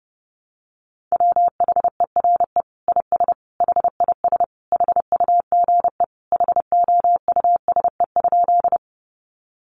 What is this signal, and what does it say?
Morse code, heard as a keyed tone, about 28 wpm, 700 Hz. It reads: WHERE IS HIS HUGE HOUSE?
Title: Sentences from Top 1100 Words 30wpm